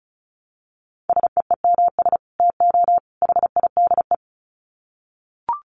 Morse code, heard as a keyed tone, about 34 wpm, 700 Hz. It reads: SEEMS TO HIDE
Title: Dit Challenge 35wpm